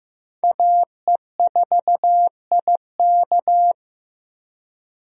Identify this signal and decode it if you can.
Morse code, heard as a keyed tone, about 15 wpm, 700 Hz. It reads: AE4IK
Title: USA Call Signs - 5 Characters - 15wpm